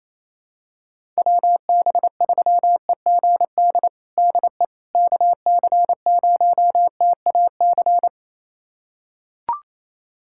Morse code, heard as a keyed tone, about 28 wpm, 700 Hz. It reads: WB3EGD DE KC0TAC
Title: USA Callsign DE Callsign - 6 Characters - 28wpm